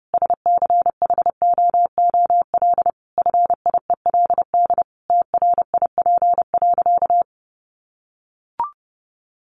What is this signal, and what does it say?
Morse code, heard as a keyed tone, about 30 wpm, 700 Hz. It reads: SCHOOL FIELD TRIP.
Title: Additional Easy Phrases 30wpm